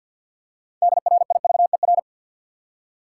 Morse code, heard as a keyed tone, about 49 wpm, 700 Hz. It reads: DRIVER
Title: Practice QSO - Exchange 3 - More Info - 50wpm